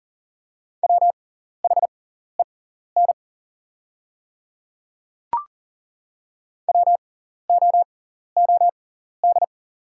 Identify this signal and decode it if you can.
Morse code, heard as a keyed tone, about 38 wpm, spaced as for 14 wpm, 700 Hz. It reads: WHEN WOOD
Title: 6x Character Spacing - Top 500 Words - 40wpm